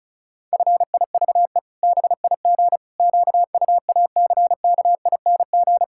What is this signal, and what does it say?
Morse code, heard as a keyed tone, about 34 wpm, 700 Hz. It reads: FIVE BIG QUACKING
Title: Pangram Challenge v2 - 35wpm